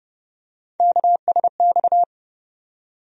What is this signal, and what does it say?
Morse code, heard as a keyed tone, about 31 wpm, 700 Hz. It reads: KSX
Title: USA Call Sign Postfixes - 3 Letters 30wpm